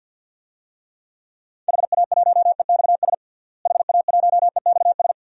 Morse code, heard as a keyed tone, about 47 wpm, 700 Hz. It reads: HA1EXS HA1EXS
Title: DXCC-100 Call Signs - 6 Characters - Repeated 3x - 50wpm